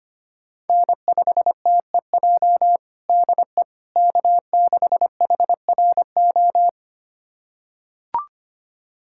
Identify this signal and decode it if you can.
Morse code, heard as a keyed tone, about 25 wpm, 700 Hz. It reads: N5TEJ DE K6HRO
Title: USA Callsign DE Callsign 25wpm